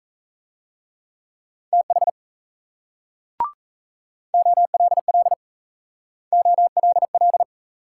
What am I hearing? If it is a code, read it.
Morse code, heard as a keyed tone, about 36 wpm, 700 Hz. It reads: TH OLL OLL